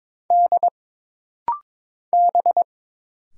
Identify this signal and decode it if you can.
Morse code, heard as a keyed tone, about 22 wpm, 700 Hz. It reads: D B
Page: This rapid-fire timing is geared to push your ICR (Instant Character Recognition) skills.